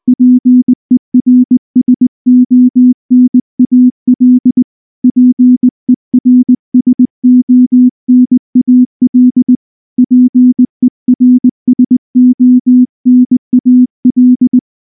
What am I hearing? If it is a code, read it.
Morse code, heard as a keyed tone, about 19 wpm, 250 Hz. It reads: PERSONAL PERSONAL PERSONAL
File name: morse / morse-training-20WPM-3